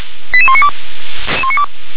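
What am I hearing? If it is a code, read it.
Five-tone selective call (ZVEI), heard as quick runs of five tones, tones of about 70 ms. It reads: 88192
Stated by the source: ZVEI five-tone selective call 88167, received on SDR.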